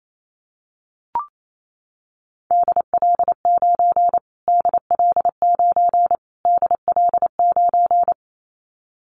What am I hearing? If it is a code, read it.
Morse code, heard as a keyed tone, about 28 wpm, 700 Hz. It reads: DL9 DL9 DL9
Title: Callsign Prefixes for Top 25 DXCC - Repeated 3x - 28wpm